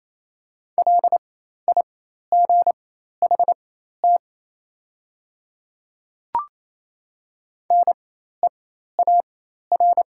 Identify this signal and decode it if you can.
Morse code, heard as a keyed tone, about 28 wpm, spaced as for 13 wpm, 700 Hz. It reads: LIGHT NEAR
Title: 4x Character Spacing - Top 300 Words - 28wpm